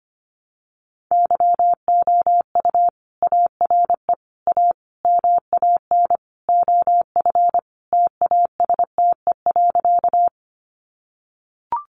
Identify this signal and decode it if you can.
Morse code, heard as a keyed tone, about 25 wpm, 700 Hz. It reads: YOU ARE A MAN OF TASTE.
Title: Sentences from Top 1100 Words 25wpm